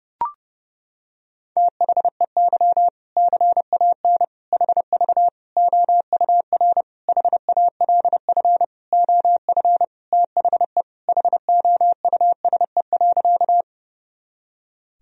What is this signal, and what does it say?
Morse code, heard as a keyed tone, about 30 wpm, 700 Hz. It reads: THEY CAN HV OUR HALF OF THE HOUSE.
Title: Sentences from Top 400 Words and Abbreviations 30wpm